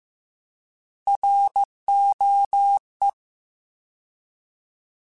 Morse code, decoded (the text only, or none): ROE